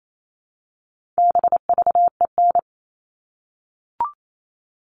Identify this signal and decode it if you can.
Morse code, heard as a keyed tone, about 28 wpm, 700 Hz. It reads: BVEN
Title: All Cut Numbers - 4-Digit 28wpm